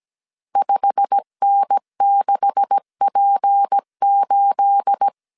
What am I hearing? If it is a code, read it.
Morse code, heard as a keyed tone, about 17 wpm, 800 Hz. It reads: 5N6P8
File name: CW / DET / 1 9 1500C S A 16wpm 2828